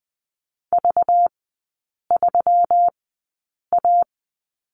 Morse code, heard as a keyed tone, about 20 wpm, spaced as for 8 wpm, 700 Hz. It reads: V3A
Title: 191 Lesson 31e - Letter v - Three Characters 20wpm